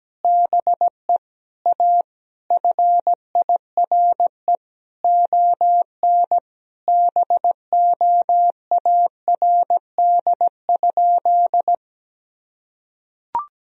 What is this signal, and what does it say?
Morse code, heard as a keyed tone, about 17 wpm, 700 Hz. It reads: BE A FIRE ON BOARD?
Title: Sentences from Top 1000 Words - Review Part 4 - 17wpm